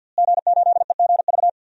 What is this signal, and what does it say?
Morse code, heard as a keyed tone, about 52 wpm, 700 Hz. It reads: G8EGV
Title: DXCC-100 Call Signs - 5 Characters - Repeated 2x - 50wpm